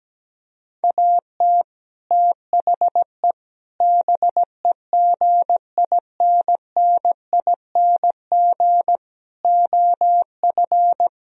Morse code, heard as a keyed tone, about 17 wpm, 700 Hz. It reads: AT THE BEGINNING OF
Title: Speed-Racing - Sets of 4 Words 17wpm